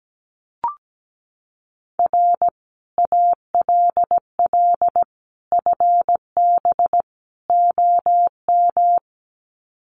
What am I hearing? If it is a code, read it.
Morse code, heard as a keyed tone, about 17 wpm, 700 Hz. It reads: R ALL FB OM
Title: Common QSO Phrases 17wpm